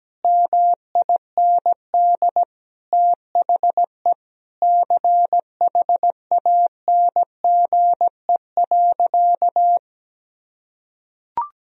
Morse code, heard as a keyed tone, about 17 wpm, 700 Hz. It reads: MIND THE CHANGE.
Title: Sentences from Top 1000 Words - Review Part 1 - 17wpm